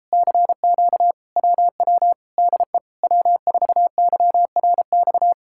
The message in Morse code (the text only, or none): CQ WW DE W4YRX